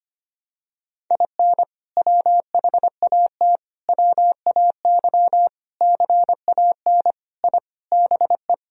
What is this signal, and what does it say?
Morse code, heard as a keyed tone, about 25 wpm, 700 Hz. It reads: IN WHAT WAY CAN I BE
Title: Sentences from Top 1000 Words - Review Part 4 - 25wpm